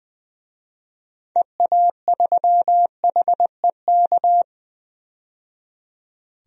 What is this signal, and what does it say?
Morse code, heard as a keyed tone, about 20 wpm, 700 Hz. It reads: EA3HEK
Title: DXCC-100 Call Signs - 6 Characters - 20wpm